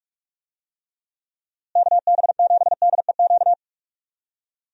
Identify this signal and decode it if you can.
Morse code, heard as a keyed tone, about 44 wpm, 700 Hz. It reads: KB7DEQ
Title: USA Call Signs - 6 Characters - Repeated 2x - 45wpm